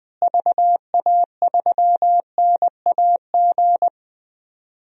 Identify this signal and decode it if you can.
Morse code, heard as a keyed tone, about 20 wpm, 700 Hz. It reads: VA3NAG
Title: DXCC-100 Call Signs - 6 Characters - Repeated 3x - 20wpm